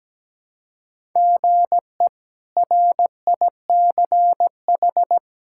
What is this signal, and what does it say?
Morse code, heard as a keyed tone, about 17 wpm, 700 Hz. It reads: GE RICH
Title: K1USN SST QSO Practice - 17wpm